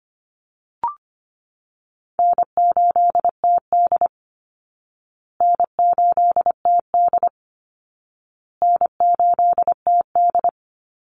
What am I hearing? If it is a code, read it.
Morse code, heard as a keyed tone, about 25 wpm, 700 Hz. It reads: N8TD N8TD N8TD
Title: Easy Call Signs Repeated 3x - 25wpm